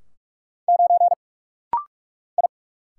Morse code, heard as a keyed tone, about 43 wpm, 690 Hz. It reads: 9 I
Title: Single Letter-Number - Rapid-Fire 45wpm